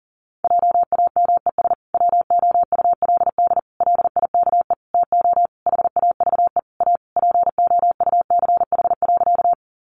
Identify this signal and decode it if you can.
Morse code, heard as a keyed tone, about 40 wpm, 700 Hz. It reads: JAMES WOULD LIKE TO HAVE A POUCH.
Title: Easy Phrases 40wpm